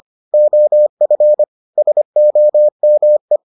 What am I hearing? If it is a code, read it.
Morse code, heard as a keyed tone, about 25 wpm, 600 Hz. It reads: OF SOME